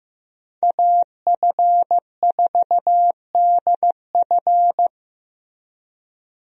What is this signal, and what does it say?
Morse code, heard as a keyed tone, about 15 wpm, 700 Hz. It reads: AF4DF
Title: USA Call Signs - 5 Characters - Repeated 2x - 15wpm